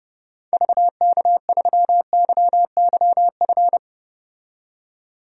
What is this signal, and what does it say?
Morse code, heard as a keyed tone, about 30 wpm, 700 Hz. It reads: VK3YYF
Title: DXCC-100 Call Signs - 6 Characters - 30wpm